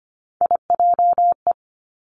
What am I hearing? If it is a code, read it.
Morse code, heard as a keyed tone, about 25 wpm, 700 Hz. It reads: IJE
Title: Three Challenging Letter-Number Combinations 25wpm